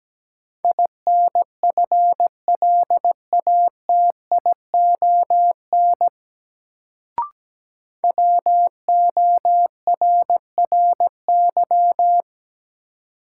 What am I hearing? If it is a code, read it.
Morse code, heard as a keyed tone, about 17 wpm, 700 Hz. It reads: INFLATION WORRY
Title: Sentences from Top 1100 Words - New Word Drill - 17wpm